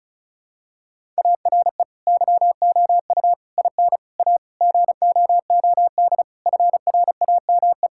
Morse code, heard as a keyed tone, about 36 wpm, 700 Hz. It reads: ARE YOU IN A GOOD FRAME
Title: Sentences from Top 1000 Words - Review Part 6 - 35wpm